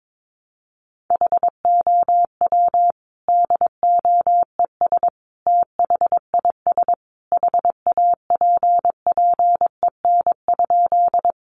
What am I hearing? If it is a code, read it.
Morse code, heard as a keyed tone, about 22 wpm, 700 Hz. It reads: HOW DOES THIS HAPPEN?
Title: Sentences from Top 400 Words 22wpm